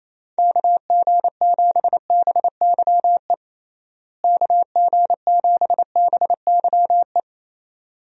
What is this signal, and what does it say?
Morse code, heard as a keyed tone, about 28 wpm, 700 Hz. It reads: KG7BYE KG7BYE
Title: USA Call Signs - 6 Characters - Repeated 2x - 28wpm